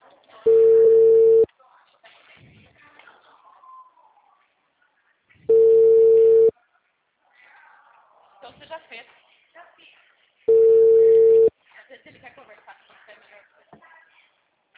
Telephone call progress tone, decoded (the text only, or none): ringback tone